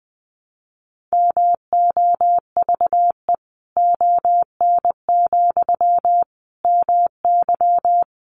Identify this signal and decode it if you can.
Morse code, heard as a keyed tone, about 20 wpm, 700 Hz. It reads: MOVE ON, MY
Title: Sentences from Top 500 Words 20wpm